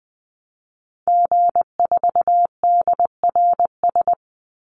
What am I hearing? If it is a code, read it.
Morse code, heard as a keyed tone, about 20 wpm, 700 Hz. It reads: G4DRS
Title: Call Signs - 20wpm